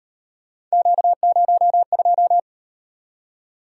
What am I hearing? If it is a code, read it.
Morse code, heard as a keyed tone, about 38 wpm, 700 Hz. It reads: Q02